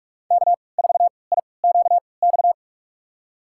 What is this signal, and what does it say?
Morse code, heard as a keyed tone, about 46 wpm, spaced as for 26 wpm, 700 Hz. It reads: K4IQX
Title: USA Call Signs - 5 Characters - 3x Character Spacing - 45wpm